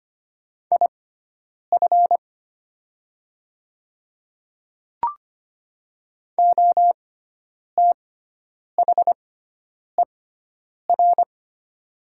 Morse code, heard as a keyed tone, about 25 wpm, spaced as for 9 wpm, 700 Hz. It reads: IF OTHER
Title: 6x Character Spacing - Top 100 Words - 25wpm